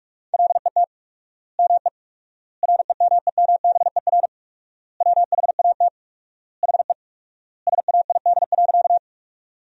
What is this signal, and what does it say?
Morse code, heard as a keyed tone, about 46 wpm, 700 Hz. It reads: LET ME REMEMBER WHAT HE SAID.
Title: 4x Word-Spacing - Sentences from Top 400 Words - 45wpm